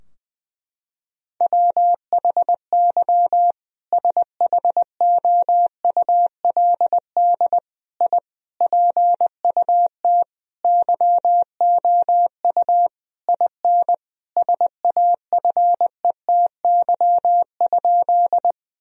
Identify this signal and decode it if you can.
Morse code, heard as a keyed tone, about 20 wpm, 700 Hz. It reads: WHY SHOULD I PUT YOU IN SAFETY?
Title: Sentences from Top 1000 Words - Review Part 4 - 20wpm